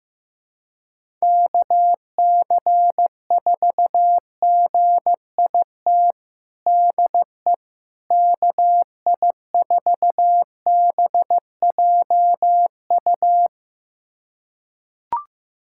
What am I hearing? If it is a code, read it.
Morse code, heard as a keyed tone, about 15 wpm, 700 Hz. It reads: KC4GIT DE KI4BJU
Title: USA Callsign DE Callsign 15wpm